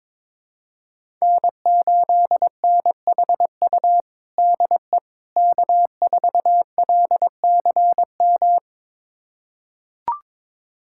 Morse code, heard as a keyed tone, about 22 wpm, 700 Hz. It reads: N8NHU DE K4LCM
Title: USA Callsign DE Callsign - 5 Characters - 22wpm